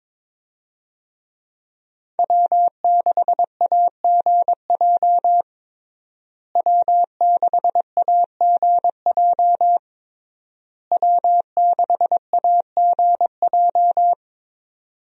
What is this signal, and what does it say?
Morse code, heard as a keyed tone, about 21 wpm, 700 Hz. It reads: W6AGJ W6AGJ W6AGJ
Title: USA Call Signs - 5 Characters - Repeated 3x - 22wpm